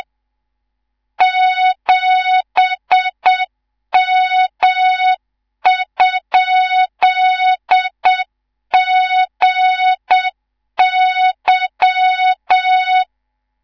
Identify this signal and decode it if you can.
Morse code, heard as a keyed tone, about 7 wpm, 3700 Hz, with noise. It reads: 7M?GY
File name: Morze-HA2MN / hang / 19 GYAKORLAT